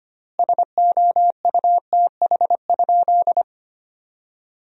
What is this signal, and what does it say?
Morse code, heard as a keyed tone, about 25 wpm, 700 Hz. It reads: SOUTH?
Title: 4x Word-Spacing - Sentences from Top 400 Words - 25wpm